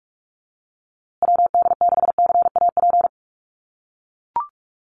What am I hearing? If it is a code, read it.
Morse code, heard as a keyed tone, about 44 wpm, 690 Hz. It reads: WD6CAP
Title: USA Call Signs Repeated 2x 45wpm